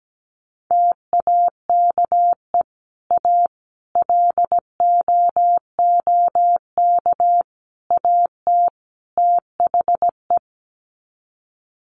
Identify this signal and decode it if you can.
Morse code, heard as a keyed tone, about 17 wpm, 700 Hz. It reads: TAKE A LOOK AT THE
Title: Sets of 5 Words 17wpm